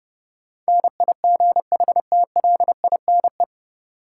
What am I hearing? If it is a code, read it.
Morse code, heard as a keyed tone, about 32 wpm, 700 Hz. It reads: NIGHTLINE
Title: Sets of 5 Words - New Word Drill - 30wpm